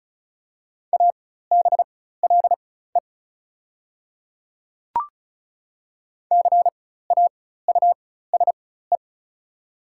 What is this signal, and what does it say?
Morse code, heard as a keyed tone, about 36 wpm, spaced as for 16 wpm, 700 Hz. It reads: ABLE CAUSE
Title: 4x Character Spacing - Top 500 Words - 35wpm